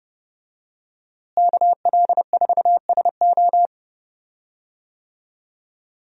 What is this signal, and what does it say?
Morse code, heard as a keyed tone, about 30 wpm, 700 Hz. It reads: KL4SO
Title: USA Call Signs - 5 Characters - Repeated 3x - 30wpm